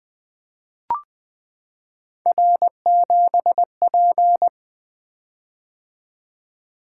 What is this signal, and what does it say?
Morse code, heard as a keyed tone, about 20 wpm, 700 Hz. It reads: R7P
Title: Three Letter-Number Combinations - 20wpm